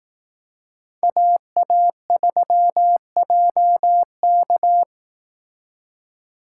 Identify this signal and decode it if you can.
Morse code, heard as a keyed tone, about 18 wpm, 700 Hz. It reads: AA3JK
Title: Speed Racing - USA 2x2 Call Signs 15wpm